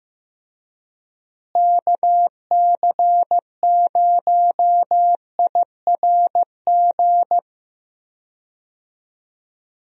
Morse code, heard as a keyed tone, about 15 wpm, 700 Hz. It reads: KC0IRG
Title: USA Call Signs - 6 Characters - Repeated 2x - 15wpm